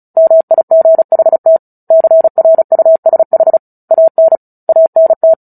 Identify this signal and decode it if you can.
Morse code, heard as a keyed tone, about 34 wpm, 650 Hz. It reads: MIGHT CRUSH AN ANT
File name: Book Two / 09-Chapter 9 - 35 WPM